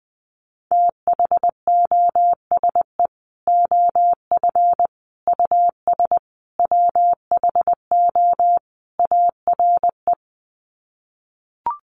Morse code, heard as a keyed tone, about 20 wpm, 700 Hz. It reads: THOSE OF US WHO ARE
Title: Sets of 5 Words Encore2 20wpm